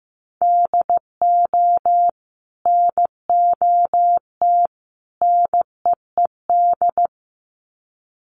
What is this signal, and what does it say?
Morse code, heard as a keyed tone, about 15 wpm, 700 Hz. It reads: DO NOT NEED
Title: Sets of 3 Words 15wpm